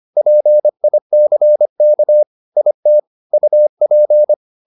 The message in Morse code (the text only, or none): PICK IT UP